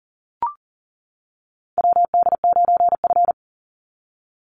Morse code, heard as a keyed tone, about 40 wpm, 700 Hz. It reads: WD9F
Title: NA SOTA Chaser Callsigns 40wpm